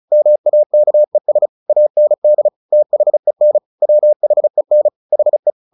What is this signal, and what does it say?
Morse code, heard as a keyed tone, about 34 wpm, 600 Hz. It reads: MAKES AND THEN WHEN HE
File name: the old man and the sea / the old man and the sea 1 35wpm